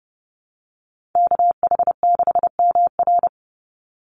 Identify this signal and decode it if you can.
Morse code, heard as a keyed tone, about 30 wpm, 700 Hz. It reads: KH6MR
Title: USA Call Signs 30wpm